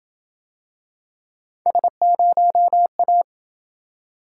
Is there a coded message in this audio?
Yes: Morse code from 1.7 to 3.2 s, heard as a keyed tone, about 27 wpm, 700 Hz. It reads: S0A